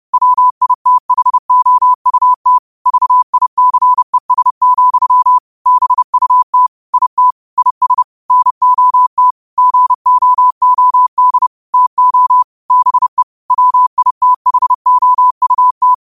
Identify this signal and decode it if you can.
Morse code, heard as a keyed tone, about 30 wpm, 1000 Hz. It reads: WITHOUT VICES, BUT IT IS NOT GOOD TO BE WITHOUT
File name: qotdfiles / 30wpm